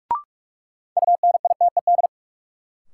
Morse code, heard as a keyed tone, about 44 wpm, 700 Hz. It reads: UNITED
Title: Sets of 2 Words - New Word Drill - 45wpm